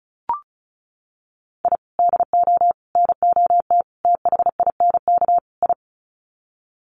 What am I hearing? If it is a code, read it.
Morse code, heard as a keyed tone, about 34 wpm, 700 Hz. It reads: I DO NOT THINK I
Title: Sets of 5 Words 35wpm